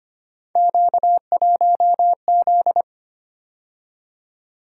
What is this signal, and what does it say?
Morse code, heard as a keyed tone, about 25 wpm, 700 Hz. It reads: Q1Z
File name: Three Letter-Number Combinations - 25wpm